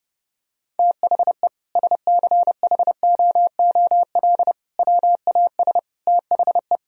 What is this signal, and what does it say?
Morse code, heard as a keyed tone, about 30 wpm, 700 Hz. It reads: THE SCHOOL WAS THE
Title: Sentences from Top 1000 Words - Review Part 2 - 30wpm